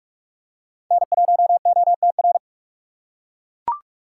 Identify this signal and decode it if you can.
Morse code, heard as a keyed tone, about 39 wpm, 700 Hz. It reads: N1OTR